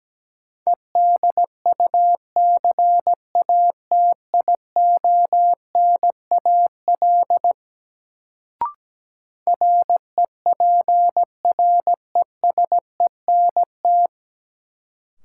Morse code, heard as a keyed tone, about 17 wpm, 700 Hz. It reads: EDUCATIONAL REPRESENT
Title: Sentences from Top 1200 Words - New Word Drill - 17wpm